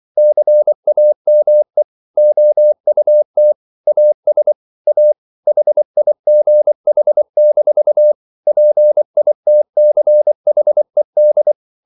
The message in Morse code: CAME OUT AS A HIGH-PITCHED

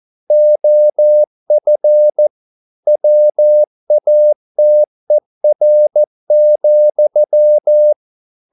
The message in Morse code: OF WATER,

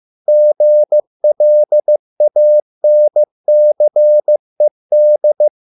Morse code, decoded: GLANCED

